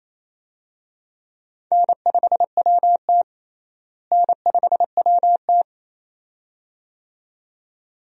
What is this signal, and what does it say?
Morse code, heard as a keyed tone, about 28 wpm, 700 Hz. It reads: N5WT N5WT
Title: USA Call Signs - 4 Characters - Repeated 2x - 28wpm